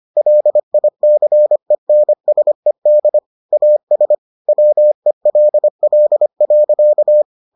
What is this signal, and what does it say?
Morse code, heard as a keyed tone, about 25 wpm, 600 Hz. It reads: LICENSED AS WELL.